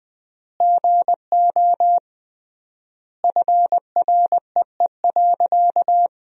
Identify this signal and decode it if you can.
Morse code, heard as a keyed tone, about 20 wpm, 700 Hz. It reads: GO FREE.
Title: 3x Word-Spacing - Sentences from Top 500 Words - 20wpm